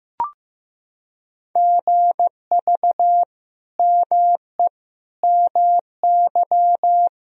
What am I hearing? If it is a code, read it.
Morse code, heard as a keyed tone, about 15 wpm, 700 Hz. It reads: GV ME MY
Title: Sentences from Top 200 Words and Abbreviations 15wpm